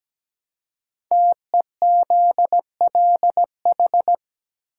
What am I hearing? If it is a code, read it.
Morse code, heard as a keyed tone, about 17 wpm, 700 Hz. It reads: TEZLH
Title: Code Groups - 5 Letters - 17wpm